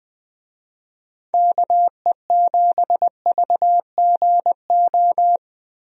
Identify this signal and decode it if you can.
Morse code, heard as a keyed tone, about 20 wpm, 700 Hz. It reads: KE7VGO
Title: USA Call Signs - 6 Characters - Repeated 3x - 20wpm